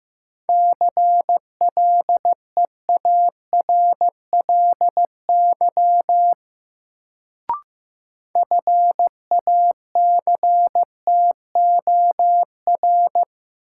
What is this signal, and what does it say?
Morse code, heard as a keyed tone, about 15 wpm, 700 Hz. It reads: CLEARLY FACTOR
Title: Sentences from Top 1300 Words - New Word Drill - 15wpm